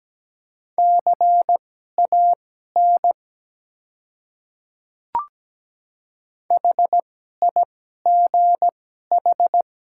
Morse code, heard as a keyed tone, about 17 wpm, spaced as for 12 wpm, 700 Hz. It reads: CAN HIGH
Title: 2x Character Spacing - Top 200 Words - 17wpm